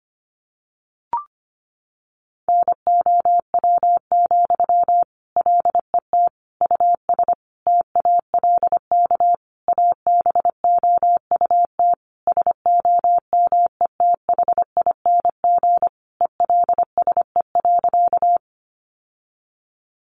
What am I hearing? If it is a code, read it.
Morse code, heard as a keyed tone, about 25 wpm, 700 Hz. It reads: NOW, LET US TALK ABOUT SOMETHING ELSE.
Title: Sentences from Top 800 Words 25wpm